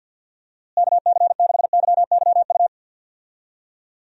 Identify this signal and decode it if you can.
Morse code, heard as a keyed tone, about 47 wpm, 700 Hz. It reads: KC6YYU
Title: USA Call Signs - 6 Characters - Repeated 2x - 50wpm